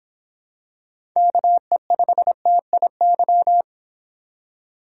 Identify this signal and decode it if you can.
Morse code, heard as a keyed tone, about 26 wpm, 700 Hz. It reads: KE5TIY